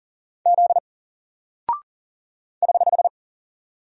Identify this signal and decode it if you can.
Morse code, heard as a keyed tone, about 39 wpm, 700 Hz. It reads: Z <HH>
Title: Single Character (Letter:Number:Punctuation:Prosign) 40wpm